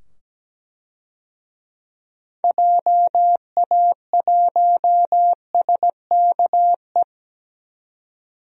Morse code, heard as a keyed tone, about 17 wpm, 700 Hz. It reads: JA1SKE
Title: DXCC-100 Call Signs - 6 Characters - 17wpm